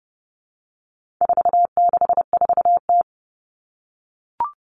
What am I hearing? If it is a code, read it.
Morse code, heard as a keyed tone, about 28 wpm, 700 Hz. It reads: 464T
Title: All Cut Numbers - 4-Digit 30wpm